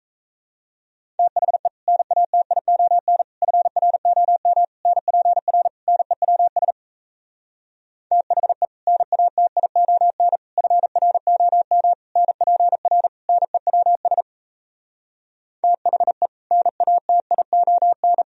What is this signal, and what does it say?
Morse code, heard as a keyed tone, about 37 wpm, 700 Hz. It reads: THE NATION FROM NPR NEWS THE NATION FROM NPR NEWS THE NATION